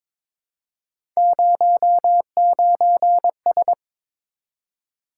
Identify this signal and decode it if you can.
Morse code, heard as a keyed tone, about 22 wpm, 700 Hz. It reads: 09S
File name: practice / 22wpm / Three Letter-Number Combinations - 22wpm